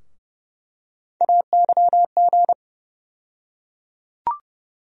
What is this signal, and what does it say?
Morse code, heard as a keyed tone, about 30 wpm, 700 Hz. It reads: AYG
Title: USA Call Sign Postfixes - 3 Letters 30wpm